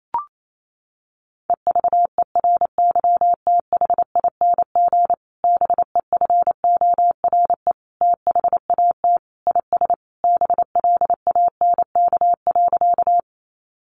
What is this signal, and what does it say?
Morse code, heard as a keyed tone, about 28 wpm, 700 Hz. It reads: EVERYTHING BEFORE THAT IS BLANK.
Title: Sentences from Top 1500 Words 28wpm